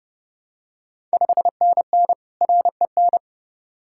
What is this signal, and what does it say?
Morse code, heard as a keyed tone, about 30 wpm, 700 Hz. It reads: 5NN REN
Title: Ontario QSO Party Exchange 30wpm